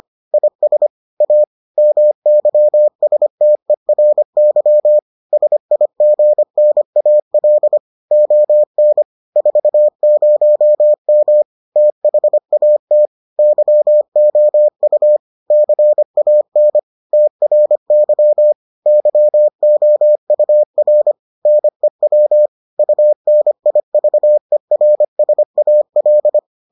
Morse code, heard as a keyed tone, about 25 wpm, 600 Hz. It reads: IS A MYSTERY SIGNAL ON 40M THAT YOU CAN TRY YOUR NEW UNIVERSAL